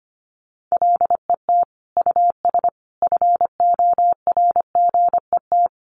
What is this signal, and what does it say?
Morse code, heard as a keyed tone, about 25 wpm, 700 Hz. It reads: LET US FORGET
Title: Sentences from Top 1200 Words 25wpm